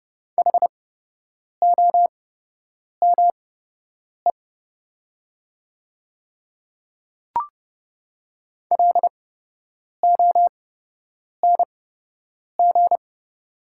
Morse code, heard as a keyed tone, about 30 wpm, spaced as for 8 wpm, 700 Hz. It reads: HOME LONG
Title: 8x Character Spacing - Top 200 Words - 30wpm